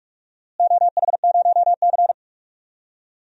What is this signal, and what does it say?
Morse code, heard as a keyed tone, about 41 wpm, 700 Hz. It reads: OH0C
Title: DXCC-100 Call Signs - 4 Characters - Repeated 2x - 45wpm